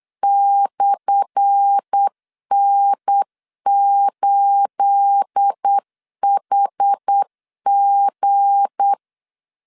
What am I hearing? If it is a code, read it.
Morse code, heard as a keyed tone, about 8 wpm, 800 Hz. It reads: /N8HG
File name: LEZIONI / 10 SIGNS / 10A SLASH 8wpm 3452